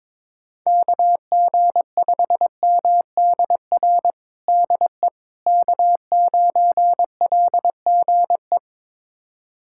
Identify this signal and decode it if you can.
Morse code, heard as a keyed tone, about 22 wpm, 700 Hz. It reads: KG5MDR DE K9LGE
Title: USA Callsign DE Callsign 22wpm